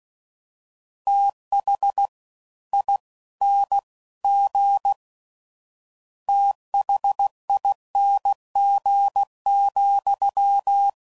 Morse code, heard as a keyed tone, about 16 wpm, 800 Hz. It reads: TH I N G THING,
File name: Prefixes / FLs & THs / Fast TH